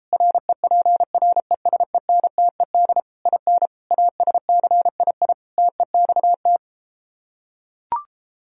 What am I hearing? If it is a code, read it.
Morse code, heard as a keyed tone, about 33 wpm, 700 Hz. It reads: REPRESENTED IN ASCII TEXT